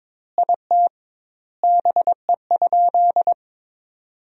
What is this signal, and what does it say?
Morse code, heard as a keyed tone, about 22 wpm, 700 Hz. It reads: IT BE?
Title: 2x Word-Spacing - Sentences from Top 300 Words - 22wpm